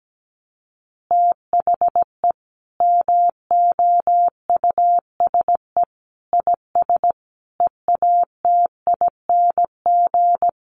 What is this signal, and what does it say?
Morse code, heard as a keyed tone, about 17 wpm, 700 Hz. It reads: THE MOUSE IS EATING
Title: Easy Phrases 17wpm